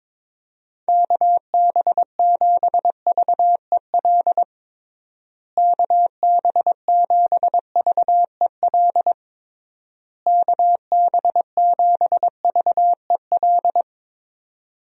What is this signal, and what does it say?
Morse code, heard as a keyed tone, about 22 wpm, 700 Hz. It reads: KB7VEL KB7VEL KB7VEL
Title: USA Call Signs - 6 Characters - Repeated 3x - 22wpm